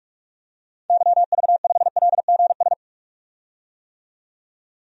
Morse code, heard as a keyed tone, about 46 wpm, 700 Hz. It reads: YV5LGS
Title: DXCC-100 Call Signs - 6 Characters - 45wpm